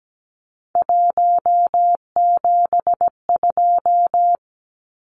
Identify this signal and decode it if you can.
Morse code, heard as a keyed tone, about 17 wpm, 700 Hz. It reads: 172
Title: Numbers - 3-Digit 17wpm